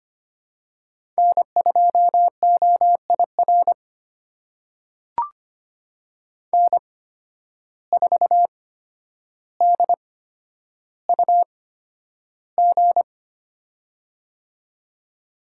Morse code, heard as a keyed tone, about 25 wpm, spaced as for 11 wpm, 700 Hz. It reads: N2OIR N4DUG
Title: USA Call Signs - 5 Characters - 8x Character Spacing - 25wpm